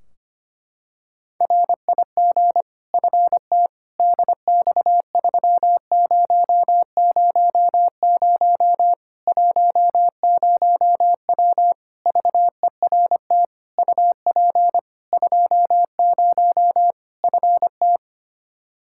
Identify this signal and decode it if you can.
Morse code, heard as a keyed tone, about 25 wpm, 700 Hz. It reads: RIG FT DX3000 10W VERT UP 20 FT
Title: Practice QSO - Exchange 2 - Info - 25wpm